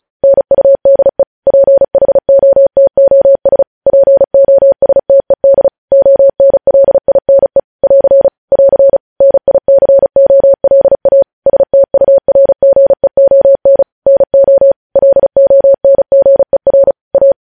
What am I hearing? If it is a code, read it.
Morse code, heard as a keyed tone, about 36 wpm, 550 Hz. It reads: NUDE PHOTOS POSTED ONLINE <AR> <AR> NICOLA STURGEON NO LONGER A
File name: podcast / News headlines-1x-35x35